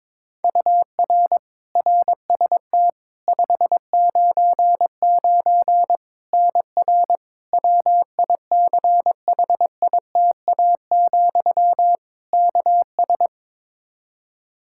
Morse code, heard as a keyed tone, about 22 wpm, 700 Hz. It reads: UR RST 599 NR WICHITA, KS
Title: K1USN SST QSO Practice 22wpm